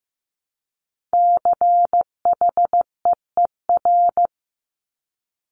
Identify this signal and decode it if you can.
Morse code, heard as a keyed tone, about 15 wpm, 700 Hz. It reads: CHEER
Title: Multi-letter Phonemes Repeated 3x 15wpm